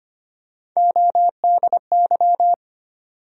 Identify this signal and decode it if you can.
Morse code, heard as a keyed tone, about 25 wpm, 700 Hz. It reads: ODY
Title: USA Call Sign Postfixes - 3 Letters 25wpm